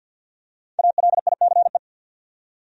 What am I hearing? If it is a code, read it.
Morse code, heard as a keyed tone, about 46 wpm, 700 Hz. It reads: ALICE
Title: USA Common Names - Max 5 Letters - 50wpm